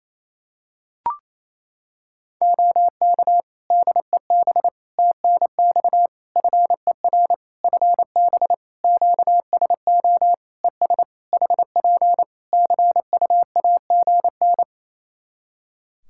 Morse code, heard as a keyed tone, about 28 wpm, 700 Hz. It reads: OK DEB TNX FER FB QSO ES HP CUAGN
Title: Practice QSO - Exchange 4 - Ending - 28wpm